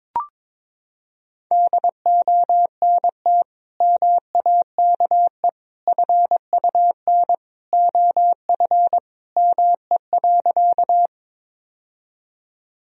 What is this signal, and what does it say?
Morse code, heard as a keyed tone, about 22 wpm, 700 Hz. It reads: DONT MAKE FUN OF ME.
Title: Sentences from Top 1000 Words - Review Part 4 - 22wpm